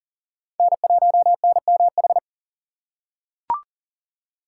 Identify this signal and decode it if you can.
Morse code, heard as a keyed tone, about 40 wpm, 700 Hz. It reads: N1NMH
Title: USA Call Signs - 5 Characters - 6x Character Spacing - 40wpm